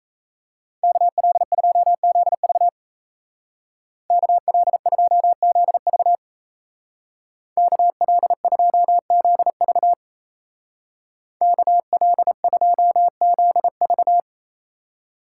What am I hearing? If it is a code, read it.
Morse code, heard as a keyed tone, about 37 wpm, 700 Hz. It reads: KL2ZV KL2ZV KL2ZV KL2ZV